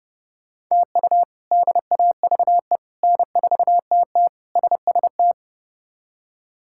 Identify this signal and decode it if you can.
Morse code, heard as a keyed tone, about 32 wpm, 700 Hz. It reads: TU DAVE N4TT SST
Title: K1USN SST QSO Practice - 30wpm